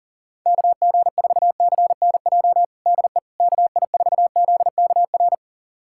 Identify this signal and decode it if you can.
Morse code, heard as a keyed tone, about 37 wpm, 700 Hz. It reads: KG4CNJ DE KI4ZKR
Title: USA Callsign DE Callsign - 6 Characters - 40wpm